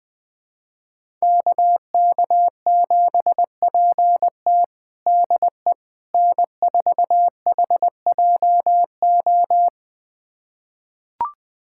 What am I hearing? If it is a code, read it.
Morse code, heard as a keyed tone, about 20 wpm, 700 Hz. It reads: KK7PT DE N4HJO
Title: USA Callsign DE Callsign - 5 Characters - 20wpm